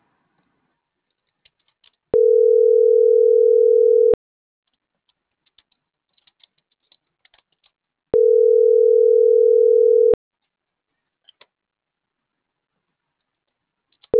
Ringback tone (North America), 440 + 480 Hz.